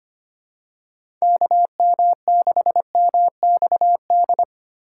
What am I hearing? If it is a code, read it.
Morse code, heard as a keyed tone, about 25 wpm, 700 Hz. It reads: KM6MXD
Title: USA Call Signs - 6 Characters - 3x Character Spacing - 25wpm